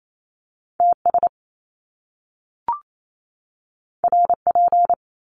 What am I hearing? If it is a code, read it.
Morse code, heard as a keyed tone, about 28 wpm, 700 Hz. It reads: TS RP
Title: Top Two Letter Combinations in Words 28wpm